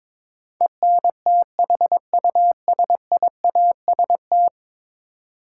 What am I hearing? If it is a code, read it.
Morse code, heard as a keyed tone, about 22 wpm, 700 Hz. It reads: ENTHUSIAST
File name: practice / 22wpm / 10 Letter Words - First 3 Letters Given - 22wpm